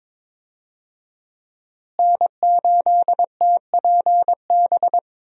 Morse code, heard as a keyed tone, about 22 wpm, 700 Hz. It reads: N8TPB
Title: USA Call Signs - 5 Characters - Repeated 3x - 22wpm